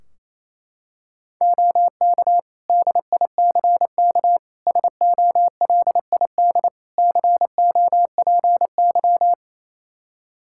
Morse code, heard as a keyed tone, about 28 wpm, 700 Hz. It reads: OK DICK SOLID COPY
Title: Practice QSO - Exchange 3 - More Info - 28wpm